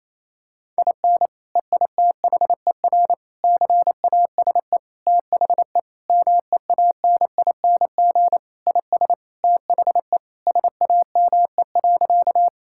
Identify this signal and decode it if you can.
Morse code, heard as a keyed tone, about 28 wpm, 700 Hz. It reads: IN EITHER CASE THE MEANING IS THE SAME.
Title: Sentences from Top 1000 Words - Review Part 6 - 28wpm